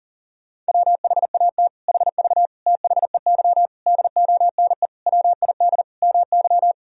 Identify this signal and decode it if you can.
Morse code, heard as a keyed tone, about 40 wpm, 700 Hz. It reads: WHAT HV THEY DONE WID MY
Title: Sentences from Top 500 Words and Abbreviations 40wpm